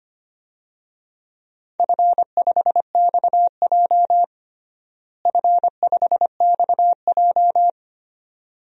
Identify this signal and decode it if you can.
Morse code, heard as a keyed tone, about 25 wpm, 700 Hz. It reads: F5XJ F5XJ
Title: DXCC-100 Call Signs - 4 Characters - Repeated 2x - 25wpm